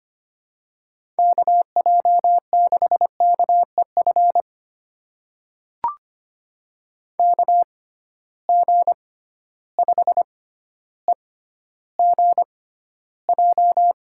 Morse code, heard as a keyed tone, about 25 wpm, spaced as for 13 wpm, 700 Hz. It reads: KJ6KEF KG5EGJ
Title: USA Call Signs - 6 Characters - 6x Character Spacing - 25wpm